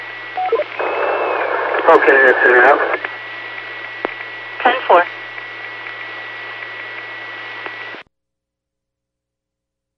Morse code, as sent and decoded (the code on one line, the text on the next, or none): none